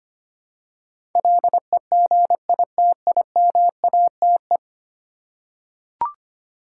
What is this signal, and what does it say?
Morse code, heard as a keyed tone, about 25 wpm, 700 Hz. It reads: LEGITIMATE
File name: 10 Letter Words - First 3 Letters Given - 25wpm